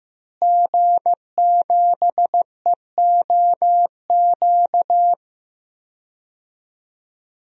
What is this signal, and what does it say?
Morse code, heard as a keyed tone, about 15 wpm, 700 Hz. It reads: G7EOQ
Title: DXCC-100 Call Signs - 5 Characters - Repeated 2x - 15wpm